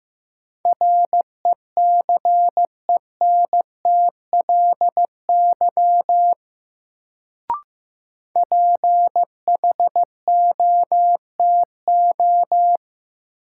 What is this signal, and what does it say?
Morse code, heard as a keyed tone, about 15 wpm, 700 Hz. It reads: RECENTLY PHOTO
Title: Sentences from Top 1200 Words - New Word Drill Alt - 15wpm